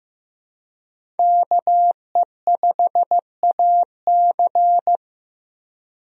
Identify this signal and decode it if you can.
Morse code, heard as a keyed tone, about 15 wpm, 700 Hz. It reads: KE5AC
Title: USA Call Signs - 5 Characters - Repeated 2x - 15wpm